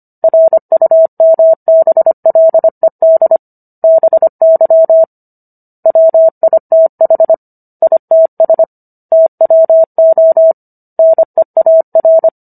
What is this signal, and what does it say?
Morse code, heard as a keyed tone, about 25 wpm, 650 Hz. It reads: RUMBLED BY WITH ITS TWO NEAR